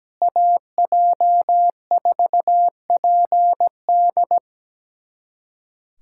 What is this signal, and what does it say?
Morse code, heard as a keyed tone, about 17 wpm, 700 Hz. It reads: AJ4PD
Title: USA Callsign DE Callsign - 5 Characters - 17wpm